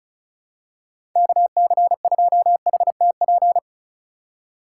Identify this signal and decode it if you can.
Morse code, heard as a keyed tone, about 36 wpm, 700 Hz. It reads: KC2HTP
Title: USA Call Signs - 6 Characters - 4x Character Spacing - 35wpm